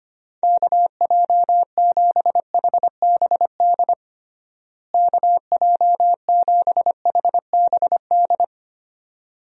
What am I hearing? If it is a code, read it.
Morse code, heard as a keyed tone, about 25 wpm, 700 Hz. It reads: KJ7HBD KJ7HBD
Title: USA Call Signs - 6 Characters - Repeated 2x - 25wpm